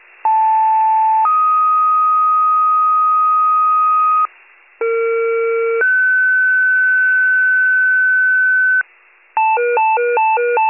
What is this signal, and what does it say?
Two-tone page, heard as a one-second tone, then a three-second tone, about 880 then 1250 Hz, about 460 then 1500 Hz, with noise. It is heard from 0.2 to 8.8 s.